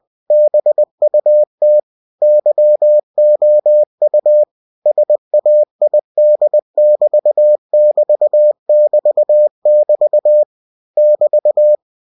Morse code, heard as a keyed tone, about 20 wpm, 600 Hz. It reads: BUT YOU SAID<BT><BT><BT><BT> <BT>
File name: cw-books / winnie-the-pooh / winnie-20wpm-page-0000